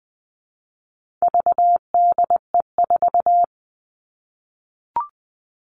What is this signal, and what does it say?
Morse code, heard as a keyed tone, about 20 wpm, 700 Hz. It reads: VDE4
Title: All Cut Numbers - 4-Digit 20wpm